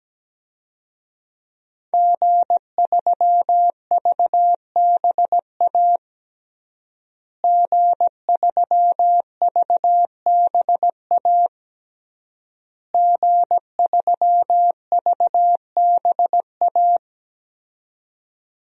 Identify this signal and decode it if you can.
Morse code, heard as a keyed tone, about 17 wpm, 700 Hz. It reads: G3VBA G3VBA G3VBA
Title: DXCC-100 Call Signs - 5 Characters - Repeated 3x - 17wpm